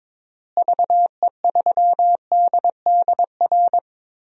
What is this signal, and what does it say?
Morse code, heard as a keyed tone, about 22 wpm, 700 Hz. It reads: VE3DDR
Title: DXCC-100 Call Signs - 6 Characters - Repeated 3x - 22wpm